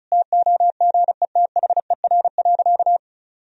Morse code, heard as a keyed tone, about 34 wpm, 700 Hz. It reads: TOGETHER.
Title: Sentences from Top 1000 Words - Review Part 1 - 35wpm